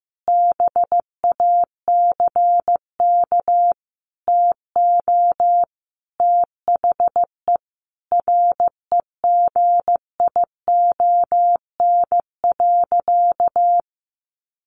Morse code, heard as a keyed tone, about 15 wpm, 700 Hz. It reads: BACK TO THE REGION.
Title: Sentences from Top 1500 Words 15wpm